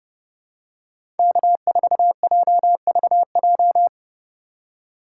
Morse code, heard as a keyed tone, about 29 wpm, 700 Hz. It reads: K4JVJ
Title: USA Call Signs - 5 Characters - 4x Character Spacing - 30wpm